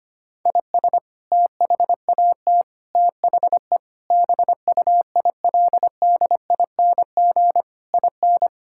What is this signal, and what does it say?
Morse code, heard as a keyed tone, about 25 wpm, 700 Hz. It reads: IS THAT THE BUILDING IN
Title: Sentences from Top 1000 Words - Review Part 5 - 25wpm